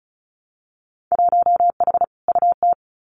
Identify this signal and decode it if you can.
Morse code, heard as a keyed tone, about 34 wpm, 700 Hz. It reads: 1H UT
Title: Winter Field Day Exchanges 35wpm